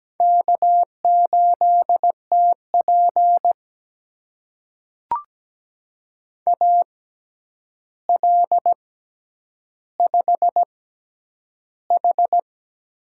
Morse code, heard as a keyed tone, about 17 wpm, spaced as for 9 wpm, 700 Hz. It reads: K8TP AL5H
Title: USA Call Signs - 6x Character Spacing - 17wpm